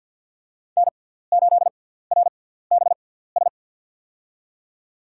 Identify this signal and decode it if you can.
Morse code, heard as a keyed tone, about 53 wpm, spaced as for 17 wpm, 700 Hz. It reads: N8RBS
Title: USA Call Signs - 6x Character Spacing - 50wpm